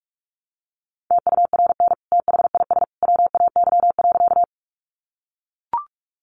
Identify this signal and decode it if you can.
Morse code, heard as a keyed tone, about 44 wpm, 700 Hz. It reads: TURN THIS WAY.
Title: Sentences from Top 200 Words 45wpm